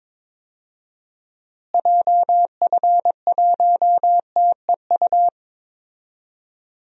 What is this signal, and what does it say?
Morse code, heard as a keyed tone, about 22 wpm, 700 Hz. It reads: JF1TEU
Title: DXCC-100 Call Signs - 6 Characters - Repeated 2x - 22wpm